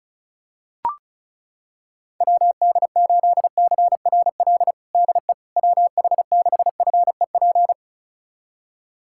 Morse code, heard as a keyed tone, about 36 wpm, 700 Hz. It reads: WD8CRL DE WH6FEP
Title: USA Callsign DE Callsign - 6 Characters - 35wpm